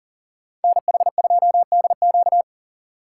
Morse code, heard as a keyed tone, about 52 wpm, 700 Hz. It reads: NH2DQ